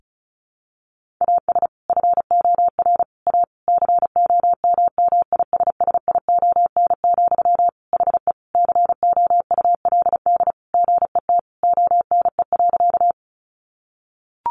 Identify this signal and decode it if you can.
Morse code, heard as a keyed tone, about 34 wpm, 700 Hz. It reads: AS FOR A COMMISSION, HE COULD GET ONE.
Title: Sentences from Top 1400 Words 35wpm